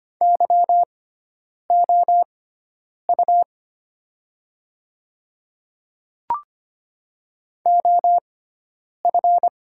Morse code, heard as a keyed tone, about 25 wpm, spaced as for 9 wpm, 700 Hz. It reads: YOU OF
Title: 6x Character Spacing - Top 200 Words - 25wpm